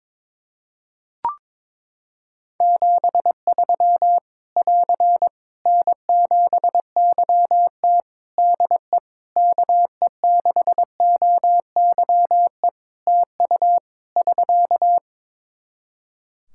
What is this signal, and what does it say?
Morse code, heard as a keyed tone, about 22 wpm, 700 Hz. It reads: 73 <AR> N7YT DE KE6OYE TU <SK>
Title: Practice QSO - Exchange 4 - Ending - 22wpm